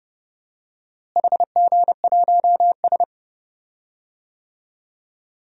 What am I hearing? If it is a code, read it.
Morse code, heard as a keyed tone, about 31 wpm, 700 Hz. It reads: HG1S
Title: DXCC-100 Call Signs - 4 Characters - Repeated 3x - 30wpm